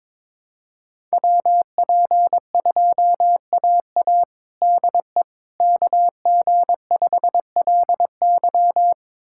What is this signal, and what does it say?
Morse code, heard as a keyed tone, about 22 wpm, 700 Hz. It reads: WP2AA DE KG5LY
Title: USA Callsign DE Callsign - 5 Characters - 22wpm